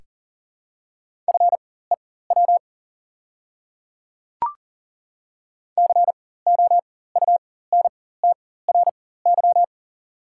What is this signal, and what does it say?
Morse code, heard as a keyed tone, about 41 wpm, spaced as for 19 wpm, 700 Hz. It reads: FEW COUNTRY
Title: 4x Character Spacing - Top 300 Words - 40wpm